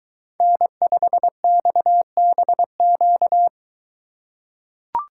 Morse code, heard as a keyed tone, about 21 wpm, 700 Hz. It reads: N5XBQ